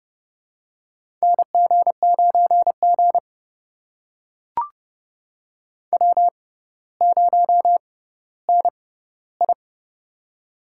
Morse code, heard as a keyed tone, about 30 wpm, spaced as for 15 wpm, 700 Hz. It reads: NG9G W0NI
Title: USA Call Signs - 4 Characters - 6x Character Spacing - 30wpm